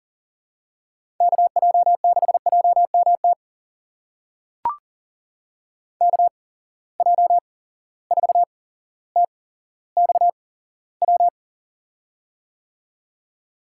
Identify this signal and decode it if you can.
Morse code, heard as a keyed tone, about 40 wpm, spaced as for 17 wpm, 700 Hz. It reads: KJ6JMT KJ4TXW
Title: USA Call Signs - 6 Characters - 8x Character Spacing - 40wpm